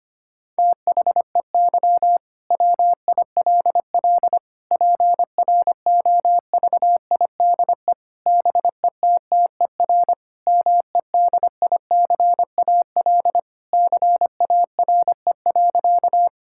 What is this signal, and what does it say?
Morse code, heard as a keyed tone, about 25 wpm, 700 Hz. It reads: THEY WILL PROVIDE BETTER MEDICAL CARE.
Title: Sentences from Top 1000 Words - Review Part 5 - 25wpm